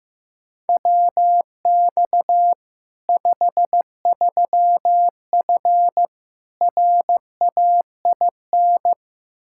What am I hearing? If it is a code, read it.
Morse code, heard as a keyed tone, about 16 wpm, 700 Hz. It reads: WX 53F RAIN
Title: Practice QSO - Exchange 2 - Info - 15wpm